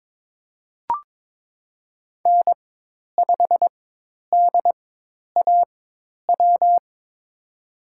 Morse code, heard as a keyed tone, about 22 wpm, spaced as for 10 wpm, 700 Hz. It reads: N5DAW
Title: USA Call Signs - 5 Characters - 4x Character Spacing - 22wpm